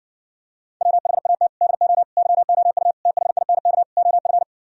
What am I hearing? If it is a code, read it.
Morse code, heard as a keyed tone, about 59 wpm, 700 Hz. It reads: WHAT DO YOU THINK OF